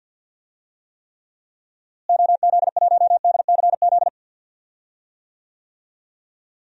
Morse code, heard as a keyed tone, about 52 wpm, 700 Hz. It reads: OZ1DCZ
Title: DXCC-100 Call Signs - 6 Characters - Repeated 2x - 50wpm